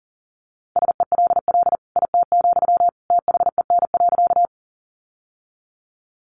Morse code, heard as a keyed tone, about 40 wpm, 700 Hz. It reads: SELL IT, THEN.
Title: Sentences from Top 1100 Words 40wpm